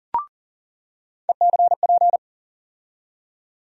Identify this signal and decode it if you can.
Morse code, heard as a keyed tone, about 39 wpm, 700 Hz. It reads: ECP
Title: USA Call Sign Postfixes - 3 Letters 40wpm